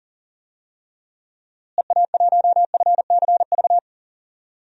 Morse code, heard as a keyed tone, about 40 wpm, 700 Hz. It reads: EA1FCV
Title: DXCC-100 Call Signs - 6 Characters - Repeated 2x - 40wpm